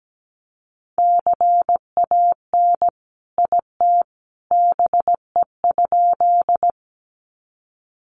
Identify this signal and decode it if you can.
Morse code, heard as a keyed tone, about 17 wpm, 700 Hz. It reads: CAN IT BE?
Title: Sentences from Top 100 Words 17wpm